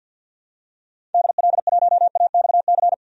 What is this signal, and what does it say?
Morse code, heard as a keyed tone, about 52 wpm, 700 Hz. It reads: DL1AXC
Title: DXCC-100 Call Signs - 6 Characters - 50wpm